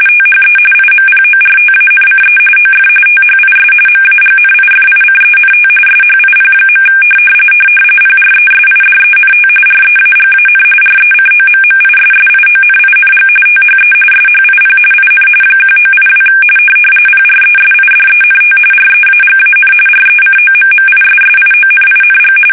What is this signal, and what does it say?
RTTY, about 45.45 baud. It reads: THE IMPORTANCE OF THIS MAGNIFICENT SHIP. PRESIDENT REAGAN RECOMMISSIONING THE SHIP IN 1982 PRESIDENT REAGAN RECOMMISSIONING
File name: fsk850 / battleshipnj